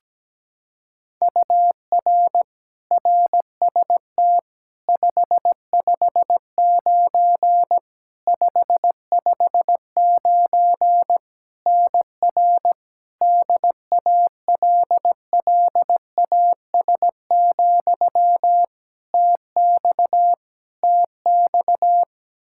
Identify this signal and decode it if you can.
Morse code, heard as a keyed tone, about 17 wpm, 700 Hz. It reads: UR RST 559 559 NR DALLAS, TX TX
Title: Practice QSO - Exchange 1 - Beginning - 17wpm